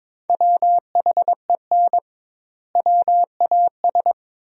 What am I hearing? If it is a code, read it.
Morse code, heard as a keyed tone, about 21 wpm, 700 Hz. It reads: WHEN WAS
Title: 2x Word-Spacing - Sentences from Top 100 Words - 22wpm